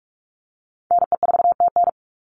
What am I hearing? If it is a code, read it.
Morse code, heard as a keyed tone, about 44 wpm, 700 Hz. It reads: NE4TN
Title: NA SOTA Chaser Callsigns 45wpm